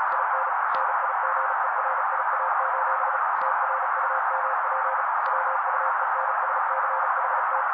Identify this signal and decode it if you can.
Morse code, heard as a keyed tone, about 37 wpm, 530 Hz. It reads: IN NIGERIA: TRUMP WUENSCHT
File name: daily / 35wpm noise